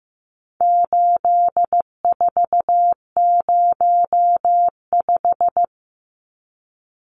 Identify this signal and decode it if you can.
Morse code, heard as a keyed tone, about 15 wpm, 700 Hz. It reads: 8405
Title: Numbers - 4-Digit 15wpm